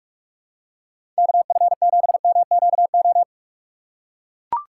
Morse code, heard as a keyed tone, about 44 wpm, 700 Hz. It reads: KF7MQO
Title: USA Call Signs - 6 Characters - 2x Character Spacing - 45wpm